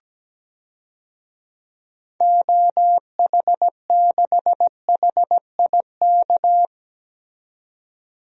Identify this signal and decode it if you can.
Morse code, heard as a keyed tone, about 17 wpm, 700 Hz. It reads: OH6HIK
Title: DXCC-100 Call Signs - 6 Characters - 17wpm